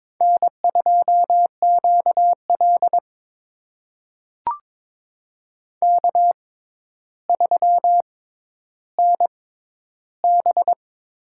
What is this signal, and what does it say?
Morse code, heard as a keyed tone, about 22 wpm, spaced as for 11 wpm, 700 Hz. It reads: N2QL K3NB
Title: USA Call Signs - 4 Characters - 6x Character Spacing - 22wpm